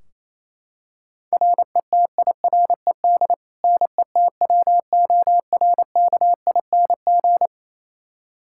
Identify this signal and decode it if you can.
Morse code, heard as a keyed tone, about 28 wpm, 700 Hz. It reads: RETIRED NETWORKING
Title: Practice QSO - Exchange 3 - More Info - 28wpm